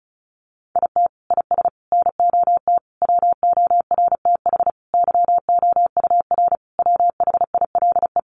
Morse code, heard as a keyed tone, about 34 wpm, 700 Hz. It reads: IT IS NOT WORTH YOUR WHILE
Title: Sentences from Top 800 Words 35wpm